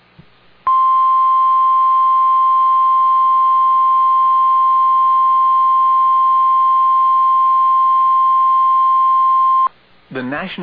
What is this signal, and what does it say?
Emergency Alert System (SAME) tones (the 1050 Hz attention tone), with speech. It is heard from 0.6 to 9.7 s.